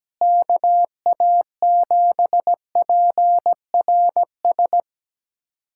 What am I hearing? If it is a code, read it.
Morse code, heard as a keyed tone, about 17 wpm, 700 Hz. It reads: KA7PRS
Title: USA Callsign DE Callsign - 6 Characters - 17wpm